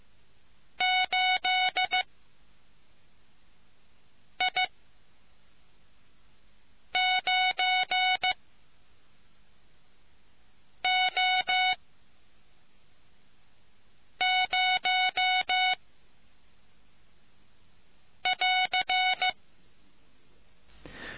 Morse code, heard as a keyed tone, about 15 wpm, spaced as for 3 wpm, 2200 Hz. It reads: 8I9O0<AR>